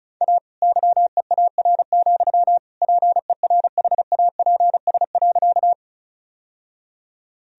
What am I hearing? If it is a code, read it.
Morse code, heard as a keyed tone, about 34 wpm, 700 Hz. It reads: A YEAR, PERHAPS.
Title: Sentences from Top 1000 Words - Review Part 1 - 35wpm